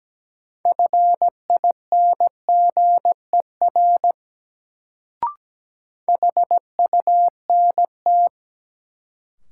Morse code, heard as a keyed tone, about 17 wpm, 700 Hz. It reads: FINGER HUNT
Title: Sentences from Top 1600 Words - New Word Drill Alt - 17wpm